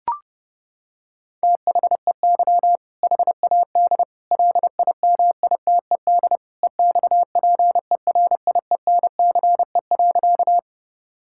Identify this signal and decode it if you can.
Morse code, heard as a keyed tone, about 30 wpm, 700 Hz. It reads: THEY HAD LIMITED EXPERIENCE.
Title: Sentences from Top 1000 Words - Review Part 6 - 30wpm